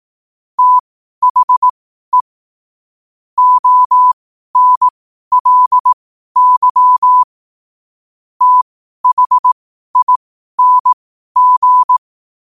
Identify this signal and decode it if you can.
Morse code, heard as a keyed tone, about 18 wpm, spaced as for 12 wpm, 1000 Hz. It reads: THE ONLY THING
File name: qotdfiles / 12wpm